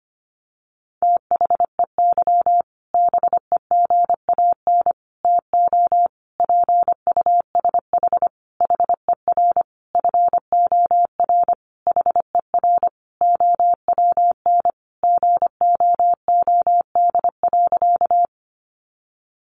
Morse code, heard as a keyed tone, about 25 wpm, 700 Hz. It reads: THEY BEGAN TO PUSH HER FOR HER OWN GOOD.
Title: Sentences from Top 1300 Words 25wpm